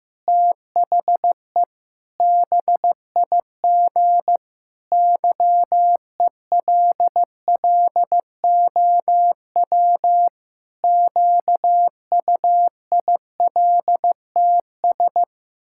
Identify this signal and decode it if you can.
Morse code, heard as a keyed tone, about 15 wpm, 700 Hz. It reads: THE BIG YELLOW QUILTS
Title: Pangram Challenge v2 - 15wpm